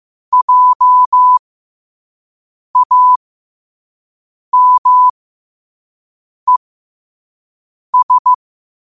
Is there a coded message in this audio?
Yes: Morse code from 0.3 to 8.3 s, heard as a keyed tone, about 15 wpm, spaced as for 5 wpm, 1000 Hz. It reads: JAMES